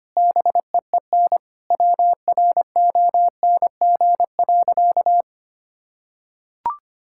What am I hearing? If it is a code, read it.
Morse code, heard as a keyed tone, about 25 wpm, 700 Hz. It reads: BEEN WRONG.
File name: practice / 25wpm / Sentences with 9-Word Phrases - 25wpm